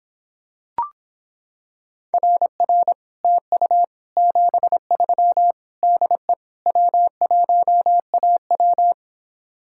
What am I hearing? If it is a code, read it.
Morse code, heard as a keyed tone, about 26 wpm, 700 Hz. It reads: RR TU 73 DE W1AW